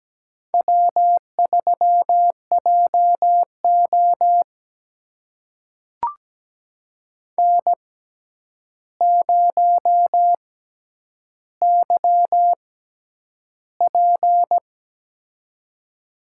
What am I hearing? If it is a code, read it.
Morse code, heard as a keyed tone, about 17 wpm, spaced as for 9 wpm, 700 Hz. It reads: W3JO N0YP
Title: USA Call Signs - 4 Characters - 6x Character Spacing - 17wpm